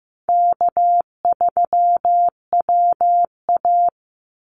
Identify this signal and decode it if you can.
Morse code, heard as a keyed tone, about 15 wpm, 700 Hz. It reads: K3WA
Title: Call Signs Repeated 2x - 15wpm